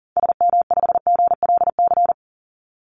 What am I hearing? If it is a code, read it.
Morse code, heard as a keyed tone, about 40 wpm, 700 Hz. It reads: SM5GLC
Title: Call Signs - 40wpm